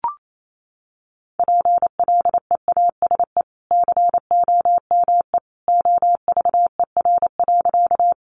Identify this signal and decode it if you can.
Morse code, heard as a keyed tone, about 28 wpm, 700 Hz. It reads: PLEASE COME OVER.
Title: Sentences from Top 1500 Words 28wpm